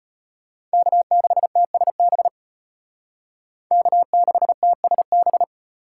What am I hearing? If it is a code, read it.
Morse code, heard as a keyed tone, about 33 wpm, 700 Hz. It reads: K6TSB K6TSB